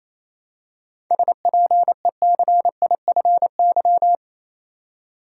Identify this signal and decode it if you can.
Morse code, heard as a keyed tone, about 28 wpm, 700 Hz. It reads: SPECIFY
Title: Seven Letter Words - 28wpm